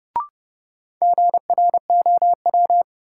Morse code, heard as a keyed tone, about 30 wpm, 700 Hz. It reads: GROW
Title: Sentences from Top 300 Words - New Word Drill - 30wpm